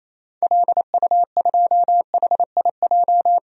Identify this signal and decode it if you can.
Morse code, heard as a keyed tone, about 28 wpm, 700 Hz. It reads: LU2HIJ
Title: DXCC-100 Call Signs - 6 Characters - 28wpm